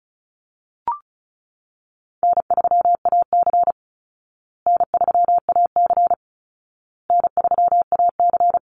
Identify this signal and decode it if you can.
Morse code, heard as a keyed tone, about 34 wpm, 700 Hz. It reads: N3AC N3AC N3AC
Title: Easy Call Signs Repeated 3x - 35wpm